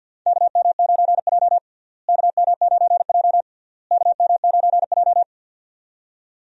Morse code, heard as a keyed tone, about 52 wpm, 700 Hz. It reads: KM9J KM9J KM9J
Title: USA Call Signs - 4 Characters - Repeated 3x - 50wpm